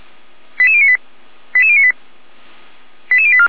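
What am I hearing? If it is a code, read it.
Five-tone selective call (ZVEI), heard as quick runs of five tones, tones of about 70 ms. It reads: 88098 88098 88084
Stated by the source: ZVEI five-tone selective call received on an SDR.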